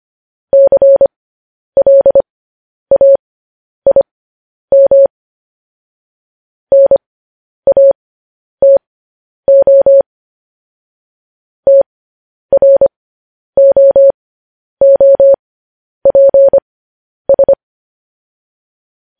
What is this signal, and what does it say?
Morse code, heard as a keyed tone, about 25 wpm, spaced as for 10 wpm, 550 Hz. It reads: CLAIM NATO TROOPS
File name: podcast / News headlines-1x-25x10